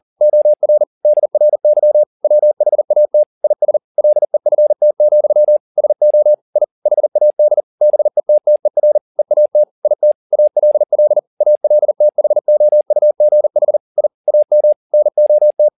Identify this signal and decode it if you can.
Morse code, heard as a keyed tone, about 40 wpm, 600 Hz. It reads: OR DRY WHAT IS LEFT, SO I HAD BETTER EAT IT ALL ALTHOUGH I AM NOT